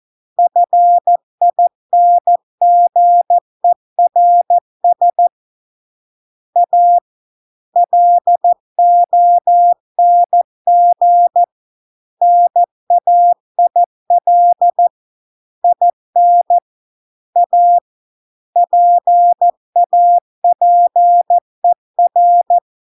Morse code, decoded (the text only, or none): FINGERS A LONG NAIL IN A PAPER